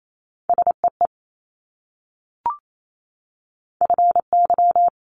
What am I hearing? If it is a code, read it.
Morse code, heard as a keyed tone, about 28 wpm, 700 Hz. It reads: SEE FY
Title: Multi-letter Phonemes Repeated 3x 28wpm